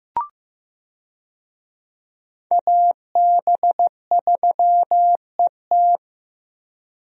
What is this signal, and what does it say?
Morse code, heard as a keyed tone, about 15 wpm, 700 Hz. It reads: AB3ET
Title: USA Call Signs - 5 Characters - 15wpm